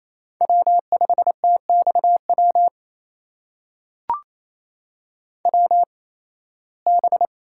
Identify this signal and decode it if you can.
Morse code, heard as a keyed tone, about 28 wpm, spaced as for 18 wpm, 700 Hz. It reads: W5TXW WB
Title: USA Call Signs - 8x Character Spacing - 28wpm